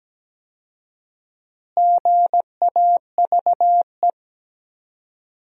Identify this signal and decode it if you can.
Morse code, heard as a keyed tone, about 17 wpm, 700 Hz. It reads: GAVE
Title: 4x Word-Spacing - Sentences from Top 500 Words - 17wpm